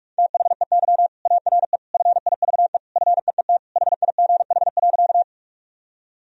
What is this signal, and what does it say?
Morse code, heard as a keyed tone, about 47 wpm, 700 Hz. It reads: THEY ARE FIVE FEET HIGH.
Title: Sentences from Top 1000 Words - Review Part 2 - 45wpm